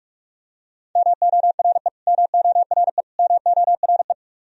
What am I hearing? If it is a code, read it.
Morse code, heard as a keyed tone, about 46 wpm, 700 Hz. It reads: MORE MORE MORE
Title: Top 100 Words Repeated 45wpm